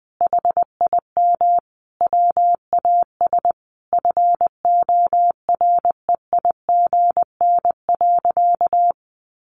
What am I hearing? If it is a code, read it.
Morse code, heard as a keyed tone, about 20 wpm, 700 Hz. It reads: HIM WAS FOREIGN.
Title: Sentences from Top 1400 Words 20wpm